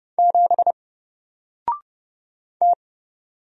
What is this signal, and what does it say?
Morse code, heard as a keyed tone, about 30 wpm, 700 Hz. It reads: 7 T
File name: Single-Letter-Number-30wpm